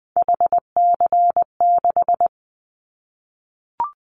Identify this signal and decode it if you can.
Morse code, heard as a keyed tone, about 19 wpm, 700 Hz. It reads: HC6
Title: Callsign Prefixes for Top 100 DXCC - 20wpm